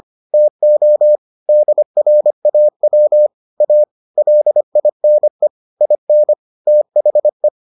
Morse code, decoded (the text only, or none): TO DRAW A LINE IN THE